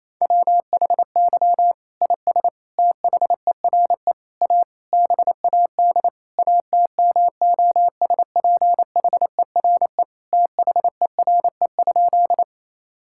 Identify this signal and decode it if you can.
Morse code, heard as a keyed tone, about 28 wpm, 700 Hz. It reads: WHY IS THERE A BAD ATMOSPHERE THERE?
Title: Sentences from Top 1600 Words 28wpm